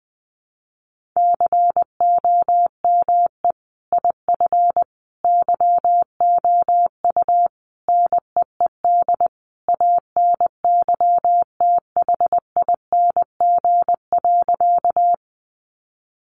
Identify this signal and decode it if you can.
Morse code, heard as a keyed tone, about 20 wpm, 700 Hz. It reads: COME IF YOU NEED ANYTHING.
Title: Sentences from Top 1000 Words 20wpm